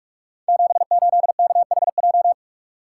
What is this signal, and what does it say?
Morse code, heard as a keyed tone, about 44 wpm, 700 Hz. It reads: 78KHJ
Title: Code Groups - 5 Letter:Numbers - 45wpm